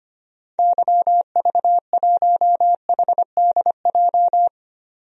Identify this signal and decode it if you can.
Morse code, heard as a keyed tone, about 25 wpm, 700 Hz. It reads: YV1HDJ
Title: DXCC-100 Call Signs - 6 Characters - 25wpm